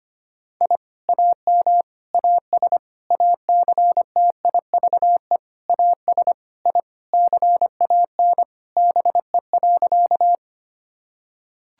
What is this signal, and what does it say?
Morse code, heard as a keyed tone, about 25 wpm, 700 Hz. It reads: I AM AS ACTIVE AS I CAN BE.
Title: Sentences from Top 1000 Words - Review Part 6 - 25wpm